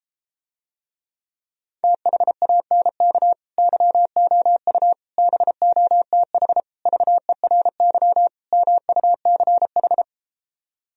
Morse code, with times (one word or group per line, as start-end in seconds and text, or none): 1.8-3.3 THANK
3.6-4.9 YOU
5.2-6.6 BOTH
6.9-8.3 VERY
8.5-10.0 MUCH